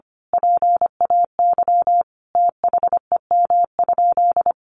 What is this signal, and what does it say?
Morse code, heard as a keyed tone, about 25 wpm, 700 Hz. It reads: PAY THEM?
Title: Sentences from Top 700 Words 25wpm